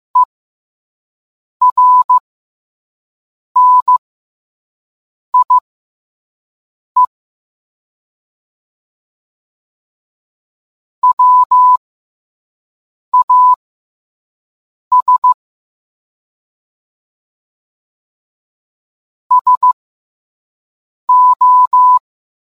Morse code, heard as a keyed tone, about 15 wpm, spaced as for 5 wpm, 1000 Hz. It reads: ERNIE WAS SO